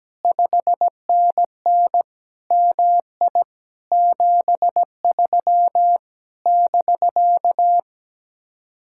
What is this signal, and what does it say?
Morse code, heard as a keyed tone, about 17 wpm, 700 Hz. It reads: 5NN MI 73 <BK>
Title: Speed-Racing Common QSO Phrases - 17wpm